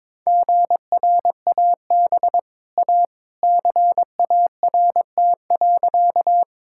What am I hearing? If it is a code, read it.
Morse code, heard as a keyed tone, about 21 wpm, 700 Hz. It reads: GRAB A CART.
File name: practice / 22wpm / Easy Sentences with Hints - 22wpm